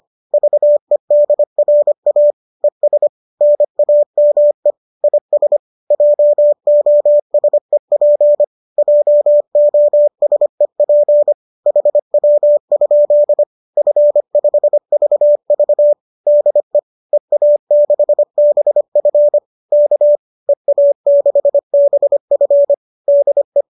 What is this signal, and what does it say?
Morse code, heard as a keyed tone, about 25 wpm, 590 Hz. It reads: VEDRA ES NAME IS JOSEP JOSEP HW? F5VV DE EA6BF K EA6BF DE